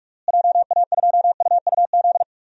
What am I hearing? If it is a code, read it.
Morse code, heard as a keyed tone, about 46 wpm, 700 Hz. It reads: JA2UUZ
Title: DXCC-100 Call Signs - 6 Characters - Repeated 3x - 45wpm